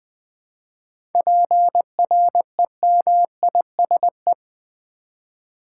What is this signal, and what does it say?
Morse code, heard as a keyed tone, about 20 wpm, 700 Hz. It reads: PREMISE
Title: 7 Letter Words - First 3 Letters Given - 20wpm